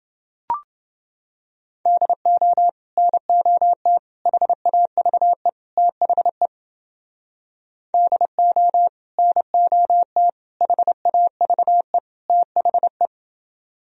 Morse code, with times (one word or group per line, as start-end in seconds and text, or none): 1.9-2.7 DO
3.0-4.0 NOT
4.3-5.5 HAVE
5.8-6.5 THE
7.9-8.9 DO
9.2-10.3 NOT
10.6-12.0 HAVE
12.3-13.1 THE